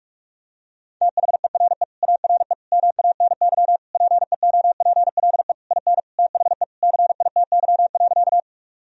Morse code, heard as a keyed tone, about 41 wpm, 700 Hz. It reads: THERE ARE MANY PEOPLE IN THE CITY.
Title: Sentences from Top 1000 Words - Review Part 1 - 45wpm